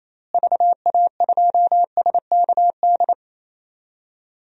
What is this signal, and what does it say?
Morse code, heard as a keyed tone, about 27 wpm, 700 Hz. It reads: VA2SKD
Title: DXCC-100 Call Signs - 6 Characters - Repeated 2x - 28wpm